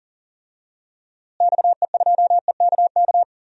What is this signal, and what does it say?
Morse code, heard as a keyed tone, about 39 wpm, 700 Hz. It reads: XE2EKK
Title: DXCC-100 Call Signs - 6 Characters - Repeated 2x - 40wpm